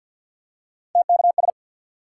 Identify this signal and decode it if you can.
Morse code, heard as a keyed tone, about 51 wpm, 700 Hz. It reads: TKS
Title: Top 90 QSO Elements 50wpm